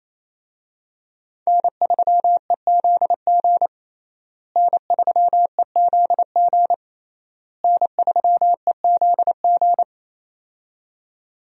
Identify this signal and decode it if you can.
Morse code, heard as a keyed tone, about 28 wpm, 700 Hz. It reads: N3EZG N3EZG N3EZG
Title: USA Call Signs - 5 Characters - Repeated 3x - 28wpm